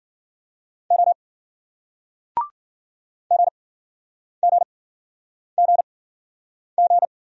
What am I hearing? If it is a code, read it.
Morse code, heard as a keyed tone, about 58 wpm, spaced as for 9 wpm, 700 Hz. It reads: O GGGG